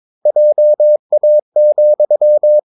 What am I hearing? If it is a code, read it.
Morse code, heard as a keyed tone, about 22 wpm, 600 Hz. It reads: JA,
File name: Ferie QRS 22wpm